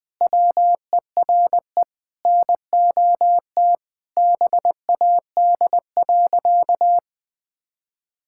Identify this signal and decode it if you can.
Morse code, heard as a keyed tone, about 19 wpm, 700 Hz. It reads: WERE NOT BAD.
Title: Sentences from Top 1000 Words - Review Part 5 - 20wpm